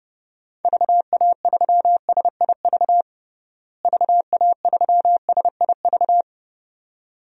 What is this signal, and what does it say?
Morse code, heard as a keyed tone, about 30 wpm, 700 Hz. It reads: VA3SIV VA3SIV
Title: DXCC-100 Call Signs - 6 Characters - Repeated 2x - 30wpm